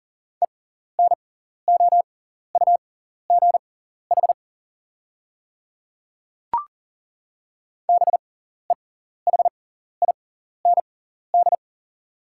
Morse code, heard as a keyed tone, about 40 wpm, spaced as for 14 wpm, 690 Hz. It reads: ENOUGH BEHIND
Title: 6x Character Spacing - Top 500 Words - 40wpm